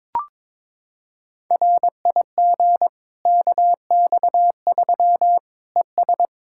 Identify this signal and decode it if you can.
Morse code, heard as a keyed tone, about 22 wpm, 700 Hz. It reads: RIG KX3 ES
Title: Practice QSO - Exchange 2 - Info - 22wpm